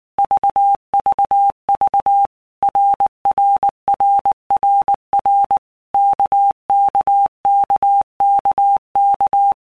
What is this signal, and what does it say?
Morse code, heard as a keyed tone, about 19 wpm, 800 Hz. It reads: VVV RRRRR KKKKK
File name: All Groups 1-minute Blocks / Groups@18 / RKPX